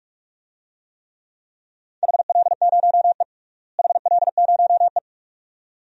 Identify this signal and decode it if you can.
Morse code, heard as a keyed tone, about 42 wpm, 700 Hz. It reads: HL0E HL0E
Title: DXCC-100 Call Signs - 4 Characters - Repeated 2x - 45wpm